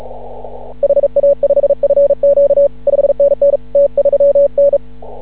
Morse code, heard as a keyed tone, about 36 wpm, 560 Hz. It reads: HA5FQ 5NN T3N
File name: CW / ha6oi 2